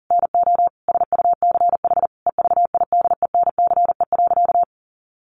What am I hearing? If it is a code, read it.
Morse code, heard as a keyed tone, about 40 wpm, 700 Hz. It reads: NO SUCH EVIDENCE.
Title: Sentences from Top 1300 Words 40wpm